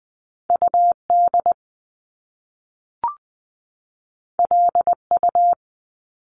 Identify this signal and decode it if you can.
Morse code, heard as a keyed tone, about 20 wpm, 700 Hz. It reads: U D L U
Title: Top Two Letter Combinations in Words 20wpm